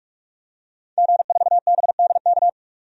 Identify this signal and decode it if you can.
Morse code, heard as a keyed tone, about 46 wpm, 700 Hz. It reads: G4BDK
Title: DXCC-100 Call Signs - 5 Characters - 45wpm